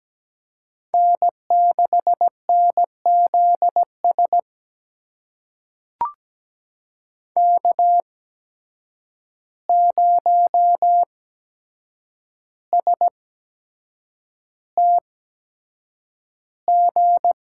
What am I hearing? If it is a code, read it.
Morse code, heard as a keyed tone, about 17 wpm, spaced as for 7 wpm, 700 Hz. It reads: N6NZS K0STG
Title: USA Call Signs - 5 Characters - 8x Character Spacing - 17wpm